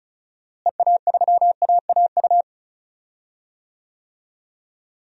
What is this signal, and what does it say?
Morse code, heard as a keyed tone, about 36 wpm, 700 Hz. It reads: EA3AAU
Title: DXCC-100 Call Signs - 6 Characters - Repeated 3x - 35wpm